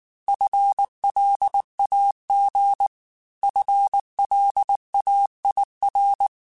Morse code, heard as a keyed tone, about 19 wpm, 800 Hz. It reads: FLAG FLAIR
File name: FL Runs / FL Runs mp3s / FL Runs @18